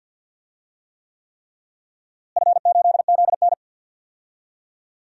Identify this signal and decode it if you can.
Morse code, heard as a keyed tone, about 53 wpm, 700 Hz. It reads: F8ZN
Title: DXCC-100 Call Signs - 4 Characters - Repeated 3x - 50wpm